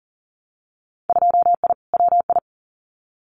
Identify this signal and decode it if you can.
Morse code, heard as a keyed tone, about 40 wpm, 700 Hz. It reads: 2I WI
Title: Winter Field Day Exchanges 40wpm